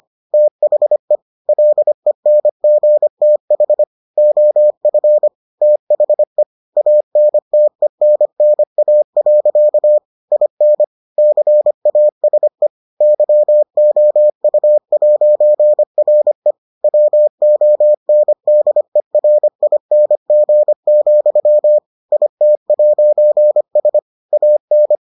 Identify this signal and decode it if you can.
Morse code, heard as a keyed tone, about 25 wpm, 600 Hz. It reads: THE LENGTH OF THE ANTENNA. IN CASE YOU'RE WONDERING, IT'S AN